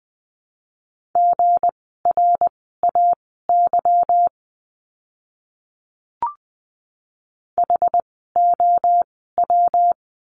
Morse code, heard as a keyed tone, about 20 wpm, spaced as for 14 wpm, 700 Hz. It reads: GRAY HOW
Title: 160 Lesson 26f - Letter g - Words 20wpm